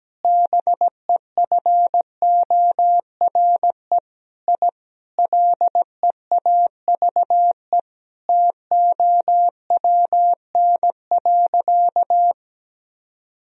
Morse code, heard as a keyed tone, about 17 wpm, 700 Hz. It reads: BEFORE I LEAVE TOWN.
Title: Sentences from Top 1000 Words - Review Part 2 - 17wpm